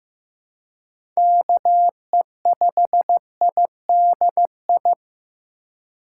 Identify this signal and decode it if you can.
Morse code, heard as a keyed tone, about 15 wpm, 700 Hz. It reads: KE5IDI
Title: USA Call Signs - 6 Characters - 2x Character Spacing - 15wpm